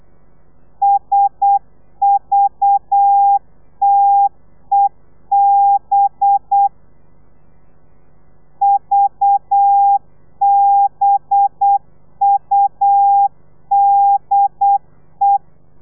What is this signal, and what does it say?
Morse code, heard as a keyed tone, about 8 wpm, 800 Hz. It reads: SVTEB VBUDE
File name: telegrafia / i2tfj / Lez4 35